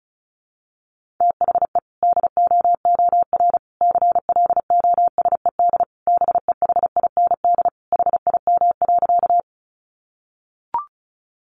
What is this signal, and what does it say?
Morse code, heard as a keyed tone, about 34 wpm, 700 Hz. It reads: THE DOOR CLOSED BEHIND HIM.
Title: Sentences from Top 1400 Words 35wpm